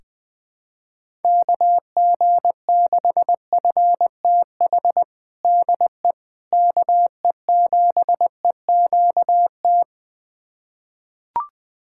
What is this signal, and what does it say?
Morse code, heard as a keyed tone, about 20 wpm, 700 Hz. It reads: KG6FTH DE KE7EQT
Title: USA Callsign DE Callsign - 6 Characters - 20wpm